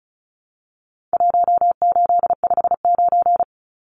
Morse code, heard as a keyed tone, about 34 wpm, 700 Hz. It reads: 1859
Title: Numbers - 4-Digit 35wpm